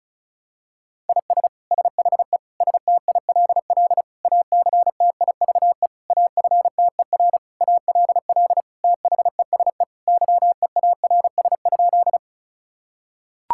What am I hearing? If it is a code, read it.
Morse code, heard as a keyed tone, about 36 wpm, 700 Hz. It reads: IS SHE STILL ACTIVE AFTER ALL THESE YEARS?
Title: Sentences from Top 1000 Words - Review Part 6 - 35wpm